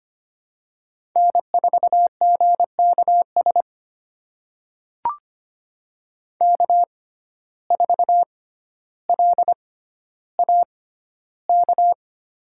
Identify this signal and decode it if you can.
Morse code, heard as a keyed tone, about 25 wpm, spaced as for 13 wpm, 700 Hz. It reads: N4GKS K4LAK
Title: USA Call Signs - 6x Character Spacing - 25wpm